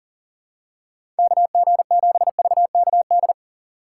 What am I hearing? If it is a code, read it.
Morse code, heard as a keyed tone, about 40 wpm, 700 Hz. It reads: KG7VKD
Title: USA Call Signs - 6 Characters - 8x Character Spacing - 40wpm